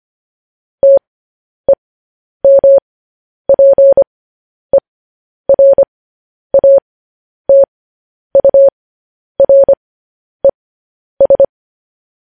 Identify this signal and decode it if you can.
Morse code, heard as a keyed tone, about 25 wpm, spaced as for 10 wpm, 550 Hz. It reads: TEMPERATURES
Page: News Headlines in Morse Code at 10 WPM